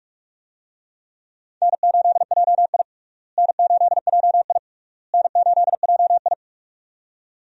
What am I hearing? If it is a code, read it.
Morse code, heard as a keyed tone, about 44 wpm, 700 Hz. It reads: N8JI N8JI N8JI
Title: USA Call Signs - 4 Characters - Repeated 3x - 45wpm